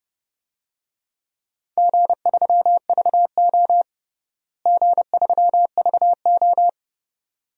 Morse code, heard as a keyed tone, about 31 wpm, 700 Hz. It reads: G3VO G3VO
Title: DXCC-100 Call Signs - 4 Characters - Repeated 2x - 30wpm